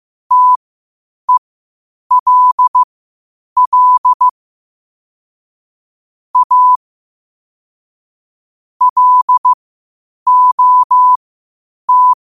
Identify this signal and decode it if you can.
Morse code, heard as a keyed tone, about 15 wpm, spaced as for 8 wpm, 1000 Hz. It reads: TELL A LOT
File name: qotdfiles / 8wpm